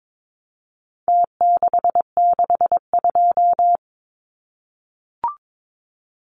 Morse code, heard as a keyed tone, about 22 wpm, 700 Hz. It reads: T662
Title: Numbers - Contest Style - 4-Digit 22wpm